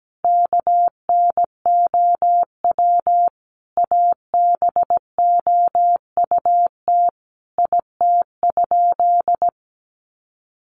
Morse code, heard as a keyed tone, about 17 wpm, 700 Hz. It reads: KNOW ABOUT IT?
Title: Sentences from Top 100 Words 17wpm